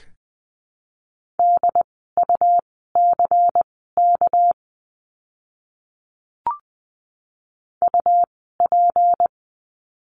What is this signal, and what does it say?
Morse code, heard as a keyed tone, about 20 wpm, spaced as for 14 wpm, 700 Hz. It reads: DUCK UP
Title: 199 Lesson 32f - Letter k - Words 20wpm